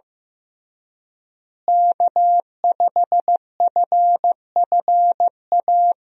Morse code, heard as a keyed tone, about 16 wpm, 700 Hz. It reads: K5FFA
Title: USA Call Signs - 5 Characters - Repeated 3x - 15wpm